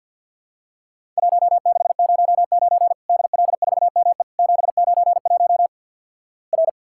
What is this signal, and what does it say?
Morse code, heard as a keyed tone, about 50 wpm, 700 Hz. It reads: 1609 DL4ME 791 R
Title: Worked All Europe DX Contest - QTC Exchange - 50wpm